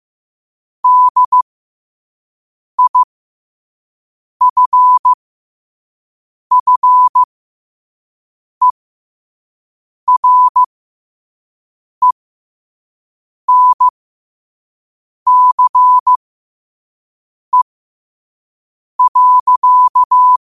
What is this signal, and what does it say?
Morse code, heard as a keyed tone, about 15 wpm, spaced as for 5 wpm, 1000 Hz. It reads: DIFFERENCE.
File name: qotdfiles / 5wpm